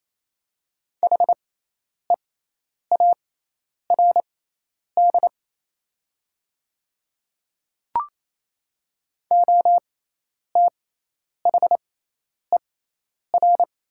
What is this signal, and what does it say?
Morse code, heard as a keyed tone, about 28 wpm, spaced as for 10 wpm, 700 Hz. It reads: HEARD OTHER
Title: 6x Character Spacing - Top 500 Words - 28wpm